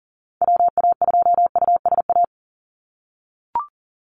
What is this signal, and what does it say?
Morse code, heard as a keyed tone, about 40 wpm, 700 Hz. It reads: WA2USA
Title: NA SOTA Activator Callsigns 40wpm